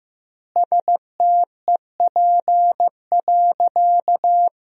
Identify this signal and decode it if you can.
Morse code, heard as a keyed tone, about 15 wpm, 700 Hz. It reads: STEP.
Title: 2x Word-Spacing - Sentences from Top 400 Words - 15wpm